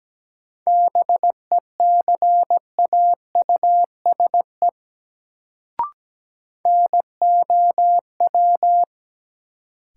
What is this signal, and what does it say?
Morse code, heard as a keyed tone, about 17 wpm, 700 Hz. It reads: BECAUSE NOW
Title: Sets of 2 Words - New Word Drill - 17wpm